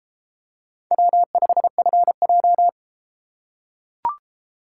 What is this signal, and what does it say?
Morse code, heard as a keyed tone, about 29 wpm, 700 Hz. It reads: W5FJ